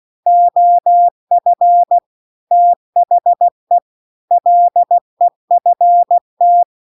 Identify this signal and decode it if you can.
Morse code, heard as a keyed tone, about 16 wpm, 700 Hz. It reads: OF THE LEFT